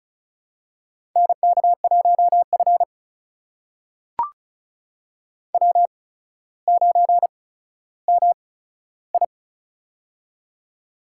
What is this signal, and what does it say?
Morse code, heard as a keyed tone, about 36 wpm, spaced as for 15 wpm, 700 Hz. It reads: NK1F W9MI
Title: USA Call Signs - 4 Characters - 8x Character Spacing - 35wpm